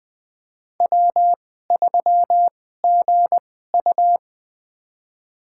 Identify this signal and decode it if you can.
Morse code, heard as a keyed tone, about 20 wpm, spaced as for 14 wpm, 700 Hz. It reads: W3GU
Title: USA Call Signs - 2x Character Spacing - 20wpm